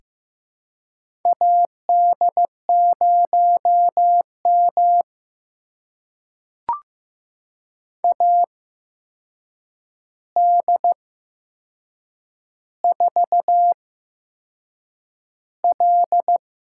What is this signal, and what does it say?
Morse code, heard as a keyed tone, about 15 wpm, spaced as for 6 wpm, 700 Hz. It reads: AD0M AD4L
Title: USA Call Signs - 4 Characters - 8x Character Spacing - 15wpm